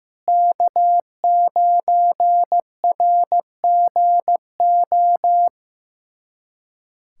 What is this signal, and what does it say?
Morse code, heard as a keyed tone, about 15 wpm, 700 Hz. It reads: K9RGO
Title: USA Callsign DE Callsign - 5 Characters - 15wpm